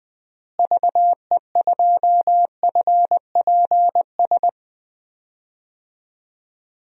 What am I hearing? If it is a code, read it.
Morse code, heard as a keyed tone, about 21 wpm, 700 Hz. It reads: VE2FPS
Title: DXCC-100 Call Signs - 6 Characters - Repeated 3x - 20wpm